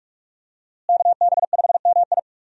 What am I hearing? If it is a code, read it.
Morse code, heard as a keyed tone, about 46 wpm, 700 Hz. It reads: KB5MI
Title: USA Call Signs - 5 Characters - Repeated 3x - 45wpm